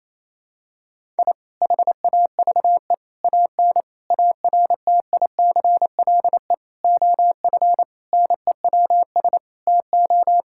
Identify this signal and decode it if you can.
Morse code, heard as a keyed tone, about 28 wpm, 700 Hz. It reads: I HAVE AN ARTICLE OF NEWS TO
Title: Sentences from Top 1000 Words - Review Part 5 - 28wpm